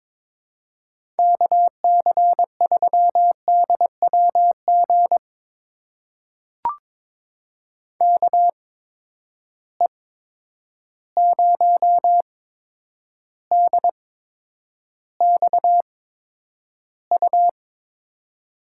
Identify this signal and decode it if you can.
Morse code, heard as a keyed tone, about 22 wpm, spaced as for 9 wpm, 700 Hz. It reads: KC3DWG KE0DXU
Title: USA Call Signs - 6 Characters - 8x Character Spacing - 22wpm